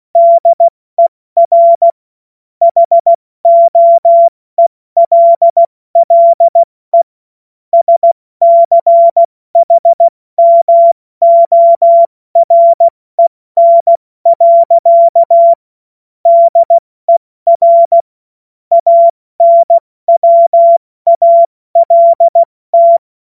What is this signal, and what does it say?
Morse code, heard as a keyed tone, about 16 wpm, spaced as for 14 wpm, 680 Hz. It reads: DER HOELLE SCHMOREN. DER ANWALT
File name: Morsekurs / Witze / Witz 09. August 2022 70 F80